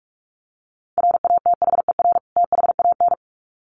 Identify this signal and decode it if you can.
Morse code, heard as a keyed tone, about 44 wpm, 700 Hz. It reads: RATHER THAN
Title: Sets of 2 Words 45wpm